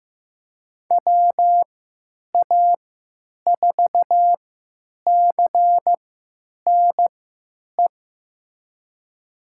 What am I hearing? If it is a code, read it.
Morse code, heard as a keyed tone, about 15 wpm, spaced as for 9 wpm, 700 Hz. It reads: WA4CNE
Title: USA Call Signs - 6 Characters - 3x Character Spacing - 15wpm